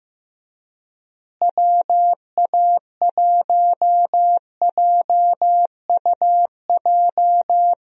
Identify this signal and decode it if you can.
Morse code, heard as a keyed tone, about 16 wpm, 700 Hz. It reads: WA1JUJ
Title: USA Call Signs - 6 Characters - Repeated 3x - 15wpm